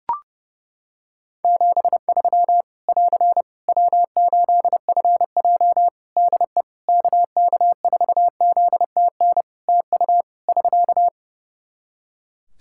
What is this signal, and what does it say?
Morse code, heard as a keyed tone, about 31 wpm, 700 Hz. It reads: 73 <AR> W8FJ DE KK4ZTN TU <SK>
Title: Practice QSO - Exchange 4 - Ending - 30wpm